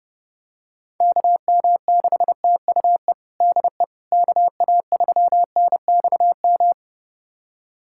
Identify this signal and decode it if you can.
Morse code, heard as a keyed tone, about 30 wpm, 700 Hz. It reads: KM6TUE DE KA3NXM
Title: USA Callsign DE Callsign 30wpm